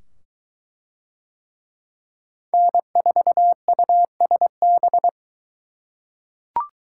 Morse code, heard as a keyed tone, about 22 wpm, 700 Hz. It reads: N4USB